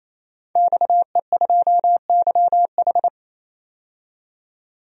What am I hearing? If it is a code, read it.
Morse code, heard as a keyed tone, about 28 wpm, 700 Hz. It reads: XE2YH
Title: DXCC-100 Call Signs - 5 Characters - 28wpm